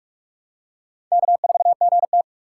Morse code, heard as a keyed tone, about 44 wpm, 700 Hz. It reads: K4GT